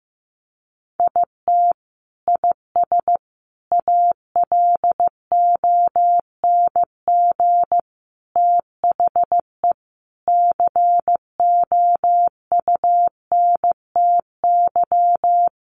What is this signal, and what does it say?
Morse code, heard as a keyed tone, about 15 wpm, 700 Hz. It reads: IT IS ALONG THE COUNTY
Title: Sentences from Top 1200 Words 15wpm